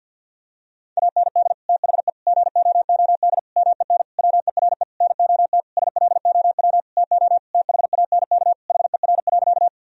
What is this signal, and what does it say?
Morse code, heard as a keyed tone, about 51 wpm, 700 Hz. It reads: AND THE GOOD MEN WERE NOT SLOW TO THANK HER.
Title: Sentences from Top 1000 Words - Review Part 2 - 50wpm